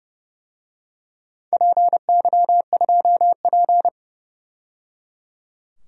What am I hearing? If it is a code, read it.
Morse code, heard as a keyed tone, about 30 wpm, 700 Hz. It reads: PY2P
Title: DXCC-100 Call Signs - 4 Characters - 30wpm